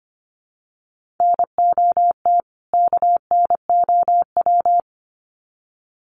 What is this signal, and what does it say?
Morse code, heard as a keyed tone, about 25 wpm, 700 Hz. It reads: NOT KNOW
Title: Sets of 2 Words 25wpm